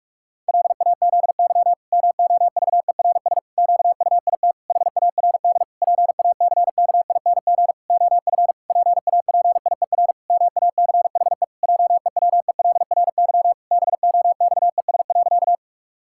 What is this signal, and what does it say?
Morse code, heard as a keyed tone, about 46 wpm, 700 Hz. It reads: LAZY MOVERS QUIT HARD PACKING OF PAPIER MACHE JEWELRY BOXES.
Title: Pangram Challenge v2 - 45wpm